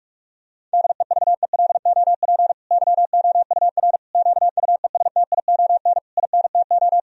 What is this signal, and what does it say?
Morse code, heard as a keyed tone, about 46 wpm, 700 Hz. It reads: DEVELOP YOUR QUESTION INTO
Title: Sentences from Top 1000 Words - Review Part 2 - 45wpm